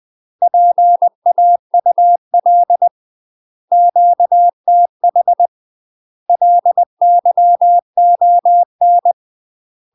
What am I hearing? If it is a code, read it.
Morse code, heard as a keyed tone, about 20 wpm, 700 Hz. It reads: PAUL QTH LYON
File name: UK MASTER QSOs TRAINING / 20WPM / UK MASTER QSOs 0039